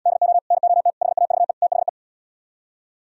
Morse code, heard as a keyed tone, about 78 wpm, 700 Hz. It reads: DO NOT HAVE THE